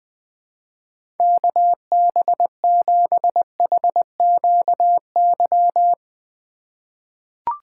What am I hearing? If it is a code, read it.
Morse code, heard as a keyed tone, about 20 wpm, 700 Hz. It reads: KB7HQY
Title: USA Call Signs - 4x Character Spacing - 20wpm